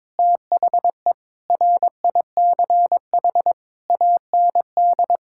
Morse code, heard as a keyed tone, about 22 wpm, 700 Hz. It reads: THE RICH AND
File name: practice / 22wpm / Sets of 8 Words - 22wpm